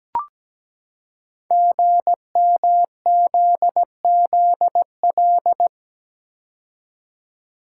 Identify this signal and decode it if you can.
Morse code, heard as a keyed tone, about 17 wpm, 700 Hz. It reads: GMZZL
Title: Code Groups - 5 Letters - 17wpm